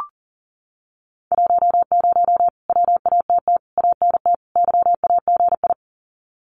Morse code, heard as a keyed tone, about 40 wpm, 700 Hz. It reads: 10 WATT ANT YAGI
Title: Common QSO Phrases Encore 40wpm